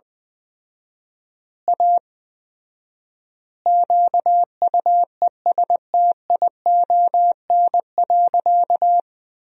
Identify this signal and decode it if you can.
Morse code, heard as a keyed tone, about 20 wpm, 700 Hz. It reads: A QUESTION.
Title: 4x Word-Spacing - Sentences from Top 400 Words - 20wpm